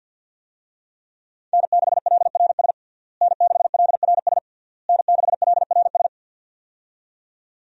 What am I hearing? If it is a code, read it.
Morse code, heard as a keyed tone, about 53 wpm, 700 Hz. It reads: N6LRS N6LRS N6LRS
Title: USA Call Signs - 5 Characters - Repeated 3x - 50wpm